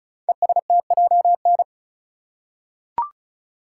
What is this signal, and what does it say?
Morse code, heard as a keyed tone, about 34 wpm, 700 Hz. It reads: ESTJN
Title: Code Groups - 5 Letter:Numbers - 35wpm